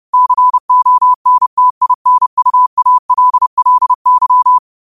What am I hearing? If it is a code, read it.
Morse code, heard as a keyed tone, about 30 wpm, 1000 Hz. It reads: CONTINUALLY
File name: qotdfiles / 30wpm